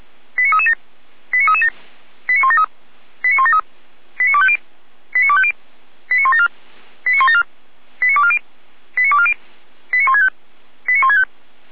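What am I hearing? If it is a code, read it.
Five-tone selective call (ZVEI), heard as quick runs of five tones, tones of about 70 ms. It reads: 89337 89337 89173 89173 89250 89250 89174 89174 89240 89240 89165 89165
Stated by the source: Received by SDR as a ZVEI five-tone selective call.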